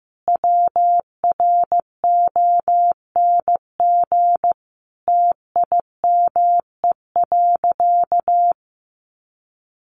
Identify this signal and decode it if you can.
Morse code, heard as a keyed tone, about 15 wpm, 700 Hz. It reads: WRONG TIME.
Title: Sentences from Top 900 Words 15wpm